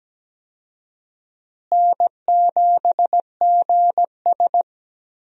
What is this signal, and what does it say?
Morse code, heard as a keyed tone, about 17 wpm, 700 Hz. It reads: N7GS
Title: USA Call Signs - 4 Characters - Repeated 2x - 17wpm